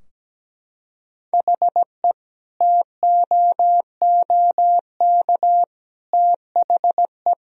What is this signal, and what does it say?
Morse code, heard as a keyed tone, about 17 wpm, 700 Hz. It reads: HE TOOK THE
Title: Sentences from Top 1000 Words - Review Part 2 - 17wpm